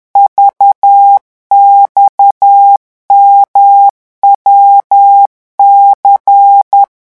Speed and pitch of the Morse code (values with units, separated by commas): 10 wpm, 800 Hz